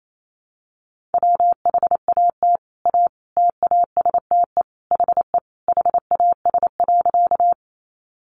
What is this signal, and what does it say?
Morse code, heard as a keyed tone, about 28 wpm, 700 Hz. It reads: WHAT A TASTE HE HAS.
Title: Sentences from Top 1100 Words 28wpm